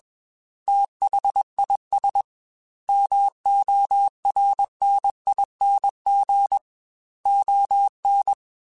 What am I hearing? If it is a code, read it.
Morse code, heard as a keyed tone, about 21 wpm, 800 Hz. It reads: THIS MORNING ON
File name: Weather Sets / Weather Sets-mp3s / Weather 4 28 @20